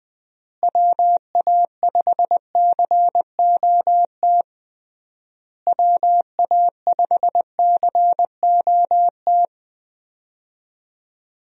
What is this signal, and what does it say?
Morse code, heard as a keyed tone, about 20 wpm, 700 Hz. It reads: WA5COT WA5COT
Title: USA Call Signs - 6 Characters - Repeated 3x - 20wpm